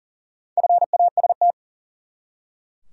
Morse code, heard as a keyed tone, about 40 wpm, 700 Hz. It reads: FAST
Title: Four Letter Words - 40wpm